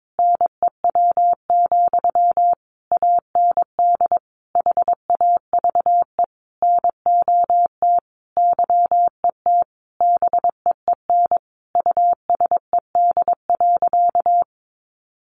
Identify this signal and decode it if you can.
Morse code, heard as a keyed tone, about 22 wpm, 700 Hz. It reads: NEW, AND HAVE NOT YET BEEN USED.
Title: Sentences from Top 700 Words 22wpm